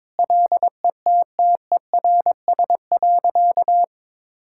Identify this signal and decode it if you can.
Morse code, heard as a keyed tone, about 22 wpm, 700 Hz. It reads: LETTERS.
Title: Sentences from Top 1000 Words - Review Part 6 - 22wpm